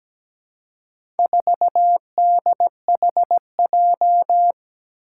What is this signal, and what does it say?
Morse code, heard as a keyed tone, about 17 wpm, 700 Hz. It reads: 4DHJ
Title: Four Challenging Letter-Number Combinations 17wpm